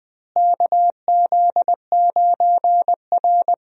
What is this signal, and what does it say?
Morse code, heard as a keyed tone, about 20 wpm, 700 Hz. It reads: KZ9R
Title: USA Call Signs - 4 Characters - Repeated 3x - 20wpm